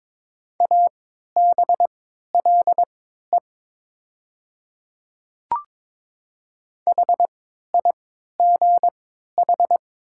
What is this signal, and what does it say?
Morse code, heard as a keyed tone, about 22 wpm, spaced as for 13 wpm, 700 Hz. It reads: ABLE HIGH
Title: 3x Character Spacing - Top 500 Words - 22wpm